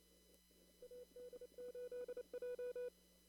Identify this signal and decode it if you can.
Morse code, heard as a keyed tone, about 28 wpm, 490 Hz. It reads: AD8J
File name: CW / kt5j wpx cw-op-k5pi